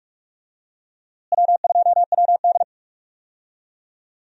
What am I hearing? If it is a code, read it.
Morse code, heard as a keyed tone, about 46 wpm, 700 Hz. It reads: W2WD
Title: USA Call Signs - 4 Characters - Repeated 2x - 45wpm